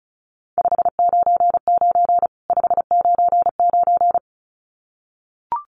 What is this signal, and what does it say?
Morse code, heard as a keyed tone, about 34 wpm, 700 Hz. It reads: 599 599
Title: Common QSO Phrases 35wpm